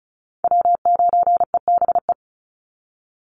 Morse code, heard as a keyed tone, about 34 wpm, 700 Hz. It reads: W9EBE
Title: Call Signs Repeated 3x - 35wpm